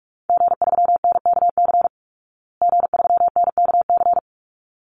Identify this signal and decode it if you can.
Morse code, heard as a keyed tone, about 44 wpm, 700 Hz. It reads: G3NKC G3NKC
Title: Call Signs Repeated 3x - 45wpm